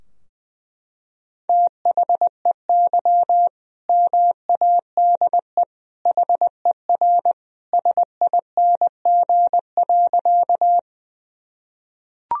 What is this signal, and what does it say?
Morse code, heard as a keyed tone, about 20 wpm, 700 Hz. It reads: THEY MADE HER SING.
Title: Sentences from Top 1000 Words - Review Part 2 - 20wpm